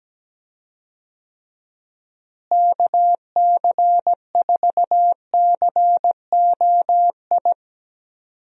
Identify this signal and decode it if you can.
Morse code, heard as a keyed tone, about 17 wpm, 700 Hz. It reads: KC4COI
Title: USA Call Signs - 6 Characters - Repeated 2x - 17wpm